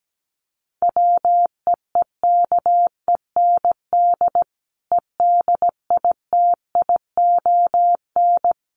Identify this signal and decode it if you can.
Morse code, heard as a keyed tone, about 17 wpm, 700 Hz. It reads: WEEKEND EDITION
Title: Sets of 5 Words 17wpm